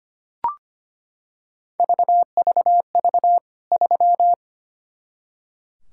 Morse code, heard as a keyed tone, about 25 wpm, 700 Hz. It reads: VVV 3
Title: K1USN SST QSO Practice 25wpm